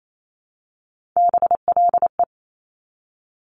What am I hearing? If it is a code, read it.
Morse code, heard as a keyed tone, about 28 wpm, 700 Hz. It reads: BLE
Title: Top Three Letter Combinations in Words 28wpm